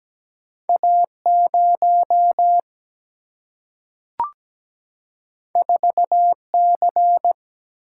Morse code, heard as a keyed tone, about 17 wpm, 700 Hz. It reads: A 0 4 C
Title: Single-Letter Single-Number 17wpm